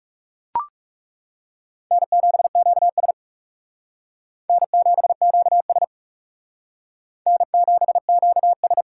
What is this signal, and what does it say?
Morse code, heard as a keyed tone, about 36 wpm, 700 Hz. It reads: N7QS N7QS N7QS
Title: Speed Racing - USA 1x2 Call Signs 30wpm